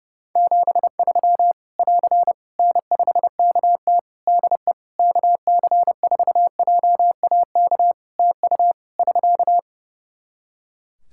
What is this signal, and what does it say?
Morse code, heard as a keyed tone, about 30 wpm, 700 Hz. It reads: 73 <AR> N5KT DE KC4JAK TU <SK>
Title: K1USN SST QSO Practice 30wpm